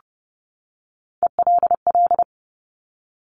Morse code, heard as a keyed tone, about 30 wpm, 690 Hz. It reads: ELL
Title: Top Three Letter Combinations in Words 30wpm